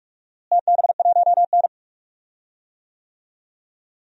Morse code, heard as a keyed tone, about 46 wpm, 700 Hz. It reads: TB1N
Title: DXCC-100 Call Signs - 4 Characters - Repeated 2x - 45wpm